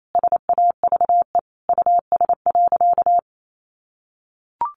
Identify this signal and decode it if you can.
Morse code, heard as a keyed tone, about 28 wpm, 700 Hz. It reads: SAVE US.
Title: Sentences from Top 1100 Words 28wpm